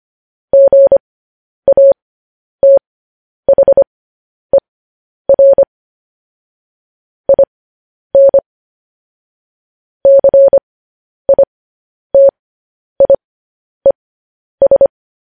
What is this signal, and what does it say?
Morse code, heard as a keyed tone, about 25 wpm, spaced as for 10 wpm, 550 Hz. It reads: GATHER IN CITIES
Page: news headlines in morse code at 10 wpm